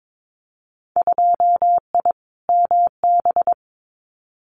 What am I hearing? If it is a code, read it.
Morse code, heard as a keyed tone, about 22 wpm, 700 Hz. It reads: 2I MB
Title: Winter Field Day Exchanges Encore 22wpm